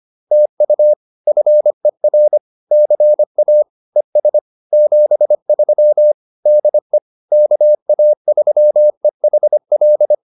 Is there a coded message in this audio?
Yes: Morse code from 0.3 to 10.1 s, heard as a keyed tone, about 25 wpm, 600 Hz. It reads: TU FER CA ES 73 DE KA3EHL